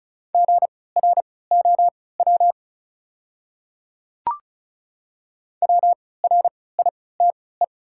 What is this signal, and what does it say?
Morse code, heard as a keyed tone, about 34 wpm, spaced as for 20 wpm, 700 Hz. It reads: GROW WRITE
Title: 3x Character Spacing - Top 500 Words - 35wpm